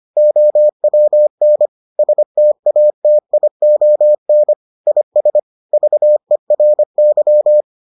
Morse code, heard as a keyed tone, about 25 wpm, 600 Hz. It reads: OWN STATION IS VERY